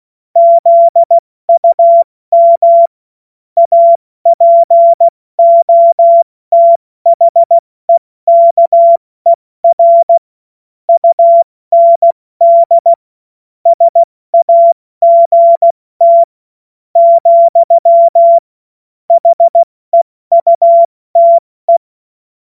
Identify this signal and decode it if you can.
Morse code, heard as a keyed tone, about 16 wpm, spaced as for 14 wpm, 680 Hz. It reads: ZUM APOTHEKER UND SAGT, HEUTE
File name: Morsekurs / Witze / Witz 05 März 2024 70 F80